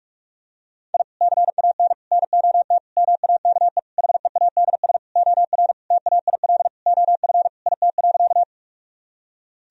Morse code, heard as a keyed tone, about 46 wpm, 700 Hz. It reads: I CAN NOT MAKE HEADS OR TAIL OF IT.
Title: Sentences from Top 1000 Words - Review Part 3 - 45wpm